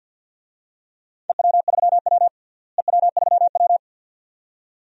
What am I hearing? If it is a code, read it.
Morse code, heard as a keyed tone, about 47 wpm, 700 Hz. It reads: EW3W EW3W
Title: DXCC-100 Call Signs - 4 Characters - Repeated 2x - 50wpm